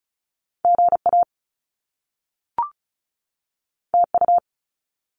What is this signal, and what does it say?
Morse code, heard as a keyed tone, about 36 wpm, 700 Hz. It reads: GA TU
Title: Top Two Letter Combinations in Words 35wpm